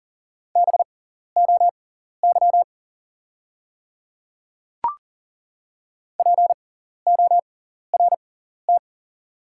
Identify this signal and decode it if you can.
Morse code, heard as a keyed tone, about 40 wpm, spaced as for 14 wpm, 700 Hz. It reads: BOY PORT
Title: 6x Character Spacing - Top 300 Words - 40wpm